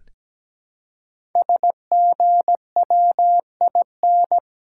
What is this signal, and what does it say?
Morse code, heard as a keyed tone, about 17 wpm, 700 Hz. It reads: SGWIN
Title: Code Groups - 5 Letters - 17wpm